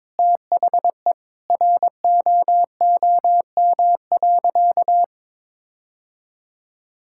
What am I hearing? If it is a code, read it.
Morse code, heard as a keyed tone, about 22 wpm, 700 Hz. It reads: THE ROOM.
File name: practice / 22wpm / Sentences with 2-Word Phrases - 22wpm